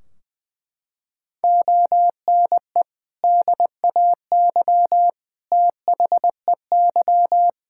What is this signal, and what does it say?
Morse code, heard as a keyed tone, about 20 wpm, 700 Hz. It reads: ONE DAY THEY
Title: Sentences from Top 1600 Words 20wpm